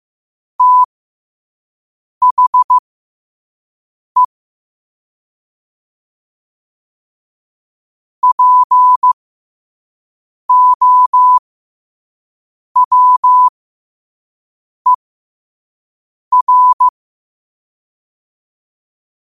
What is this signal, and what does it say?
Morse code, heard as a keyed tone, about 15 wpm, spaced as for 5 wpm, 1000 Hz. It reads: THE POWER
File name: 5wpm